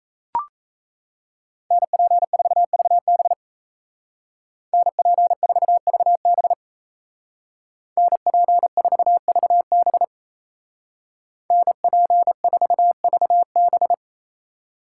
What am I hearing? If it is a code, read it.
Morse code, heard as a keyed tone, about 37 wpm, 700 Hz. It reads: NP4VB NP4VB NP4VB NP4VB